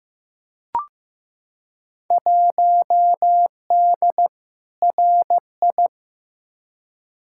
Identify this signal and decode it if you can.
Morse code, heard as a keyed tone, about 15 wpm, 700 Hz. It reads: 1D RI
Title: Field Day Exchanges - 15wpm